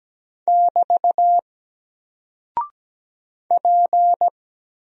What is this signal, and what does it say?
Morse code, heard as a keyed tone, about 17 wpm, 700 Hz. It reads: <BT> P
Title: Single Character (Letter:Number:Punctuation:Prosign) Alt 17wpm